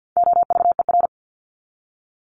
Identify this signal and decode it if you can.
Morse code, heard as a keyed tone, about 50 wpm, 700 Hz. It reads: OVER
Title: Top 500 Words 50wpm